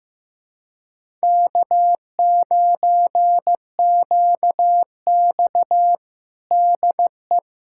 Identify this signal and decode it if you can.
Morse code, heard as a keyed tone, about 15 wpm, 700 Hz. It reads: K9QX DE
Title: USA Callsign DE Callsign - 4 Characters - 15wpm